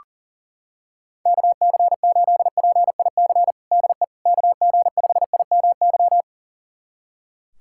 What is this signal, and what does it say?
Morse code, heard as a keyed tone, about 40 wpm, 700 Hz. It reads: KC8PIC DE KG5IMY
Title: USA Callsign DE Callsign - 6 Characters - 40wpm